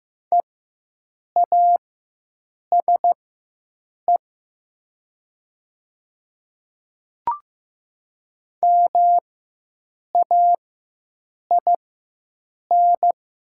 Morse code, heard as a keyed tone, about 15 wpm, spaced as for 7 wpm, 690 Hz. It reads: EASE MAIN
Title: 4x Character Spacing - Top 500 Words - 15wpm